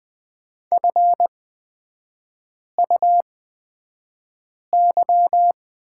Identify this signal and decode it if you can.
Morse code, heard as a keyed tone, about 20 wpm, spaced as for 5 wpm, 690 Hz. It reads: FUY
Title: 229 Lesson 37e - Letter x - Three Characters 20wpm